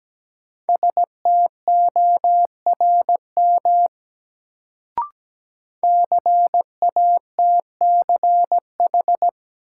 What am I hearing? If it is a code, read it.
Morse code, heard as a keyed tone, about 17 wpm, 700 Hz. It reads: STORM CATCH
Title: Sentences from Top 1500 Words - New Word Drill - 17wpm